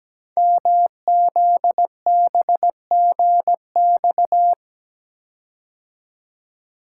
Morse code, heard as a keyed tone, about 17 wpm, 700 Hz. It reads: MZBGX
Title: Code Groups - 5 Letters - 17wpm